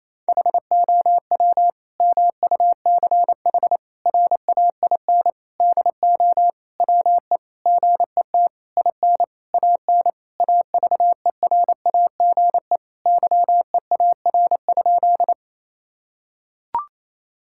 Morse code, heard as a keyed tone, about 28 wpm, 700 Hz. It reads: HOW MUCH RAIN DO WE GET IN AN AVERAGE YEAR?
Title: Sentences from Top 1000 Words - Review Part 6 - 28wpm